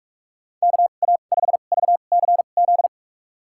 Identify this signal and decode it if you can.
Morse code, heard as a keyed tone, about 46 wpm, spaced as for 33 wpm, 700 Hz. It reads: KA5VCZ
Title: USA Call Signs - 2x Character Spacing - 45wpm